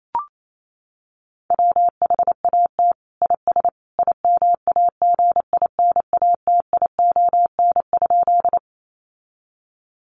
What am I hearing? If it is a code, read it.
Morse code, heard as a keyed tone, about 28 wpm, 700 Hz. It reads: WHAT IS IMAGINATION?
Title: Sentences from Top 1300 Words 28wpm